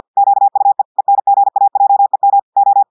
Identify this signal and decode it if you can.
Morse code, heard as a keyed tone, about 50 wpm, 800 Hz. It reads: QUE ENGAJEM O
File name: portugais texte en clair 2 vitesse 50 mots minutes